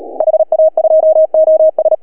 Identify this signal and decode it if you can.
Morse code, heard as a keyed tone, about 36 wpm, 660 Hz. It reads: HA2OS
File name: CW / HA2OS RHOB 2017 CW